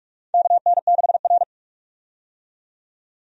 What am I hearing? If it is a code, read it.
Morse code, heard as a keyed tone, about 46 wpm, 700 Hz. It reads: KN6R
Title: USA Call Signs - 4 Characters - 45wpm